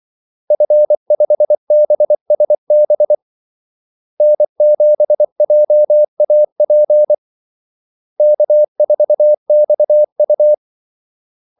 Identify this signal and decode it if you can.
Morse code, heard as a keyed tone, about 24 wpm, 600 Hz. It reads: F5BSB N7JAP K4XU
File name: Callsigns-24wpm